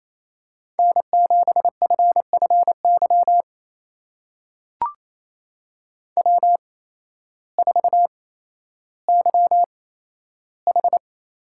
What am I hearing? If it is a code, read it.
Morse code, heard as a keyed tone, about 28 wpm, spaced as for 13 wpm, 700 Hz. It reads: N7FFY W4YH
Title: USA Call Signs - 5 Characters - 8x Character Spacing - 28wpm